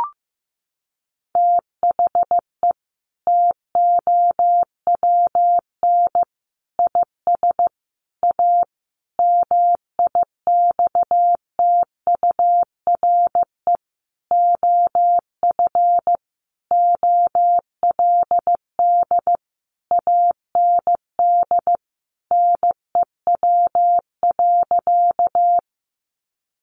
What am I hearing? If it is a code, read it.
Morse code, heard as a keyed tone, about 15 wpm, 700 Hz. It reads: THE TOWN IS A MIXTURE OF OLD AND NEW.
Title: Sentences from Top 1500 Words 15wpm